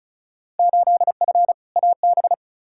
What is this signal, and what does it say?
Morse code, heard as a keyed tone, about 36 wpm, 700 Hz. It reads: 8F AB
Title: Field Day Exchanges Encore - 35wpm